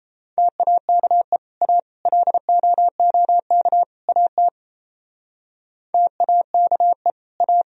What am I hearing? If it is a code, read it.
Morse code, heard as a keyed tone, about 35 wpm, 700 Hz. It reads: TAKE A LOOK AT TAKE A